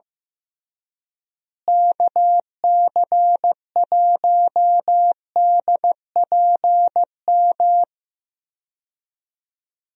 Morse code, heard as a keyed tone, about 15 wpm, 700 Hz. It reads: KC1DPM
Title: USA Call Signs - 6 Characters - Repeated 3x - 15wpm